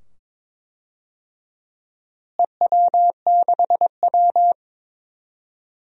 Morse code, heard as a keyed tone, about 22 wpm, 700 Hz. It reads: EW6W
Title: DXCC-100 Call Signs - 4 Characters - Repeated 2x - 22wpm